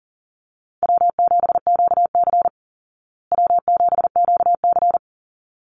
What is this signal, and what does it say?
Morse code, heard as a keyed tone, about 40 wpm, 700 Hz. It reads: W7QC W7QC
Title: Call Signs Repeated 3x - 40wpm